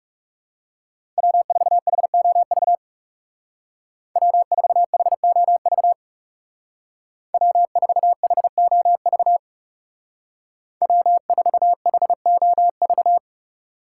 Morse code, heard as a keyed tone, about 39 wpm, 700 Hz. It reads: W4HOV W4HOV W4HOV W4HOV